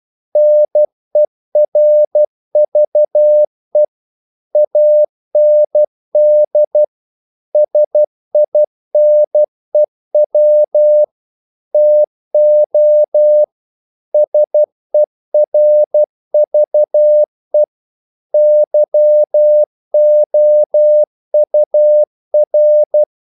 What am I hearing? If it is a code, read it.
Morse code, heard as a keyed tone, about 12 wpm, 600 Hz. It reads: NERVE AND SINEW TO SERVE YOUR
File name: CW / Practice / If12